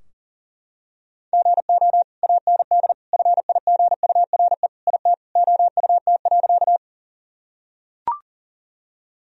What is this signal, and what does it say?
Morse code, heard as a keyed tone, about 40 wpm, 700 Hz. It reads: GO AND FIGURE IT OUT.
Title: Sentences from Top 1000 Words - Review Part 2 - 40wpm